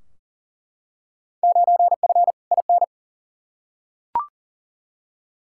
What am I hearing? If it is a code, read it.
Morse code, heard as a keyed tone, about 40 wpm, 700 Hz. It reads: 9F IN
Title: Field Day Exchanges Encore - 40wpm